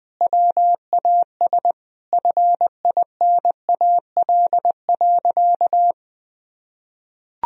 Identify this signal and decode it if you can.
Morse code, heard as a keyed tone, about 21 wpm, 700 Hz. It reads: WAS FINAL.
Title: Sentences from Top 1000 Words - Review Part 2 - 20wpm